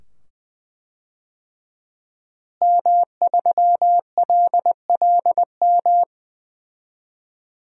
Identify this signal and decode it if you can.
Morse code, heard as a keyed tone, about 21 wpm, 700 Hz. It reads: M3LLM
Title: DXCC-100 Call Signs - 5 Characters - Repeated 2x - 20wpm